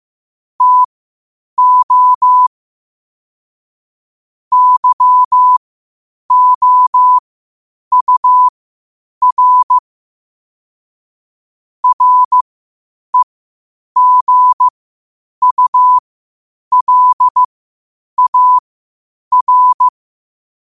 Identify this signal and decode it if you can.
Morse code, heard as a keyed tone, about 15 wpm, spaced as for 8 wpm, 1000 Hz. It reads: TO YOUR REGULAR
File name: qotdfiles / 8wpm